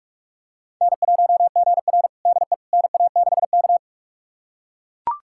Morse code, heard as a keyed tone, about 43 wpm, 700 Hz. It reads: N1GR DE NA6K
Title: USA Callsign DE Callsign - 4 Characters - 45wpm